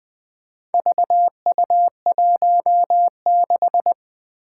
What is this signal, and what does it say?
Morse code, heard as a keyed tone, about 20 wpm, 700 Hz. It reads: VU16
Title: Four Challenging Letter-Number Combinations 20wpm